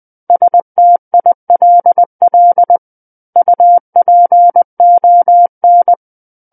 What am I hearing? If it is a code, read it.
Morse code, heard as a keyed tone, about 20 wpm, 700 Hz. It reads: STILL UPON
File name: 20 WPM